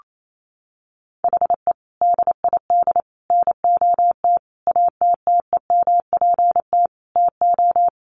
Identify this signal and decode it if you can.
Morse code, heard as a keyed tone, about 28 wpm, 700 Hz. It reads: HE DID NOT ATTEMPT TO
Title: Sentences from Top 1400 Words 28wpm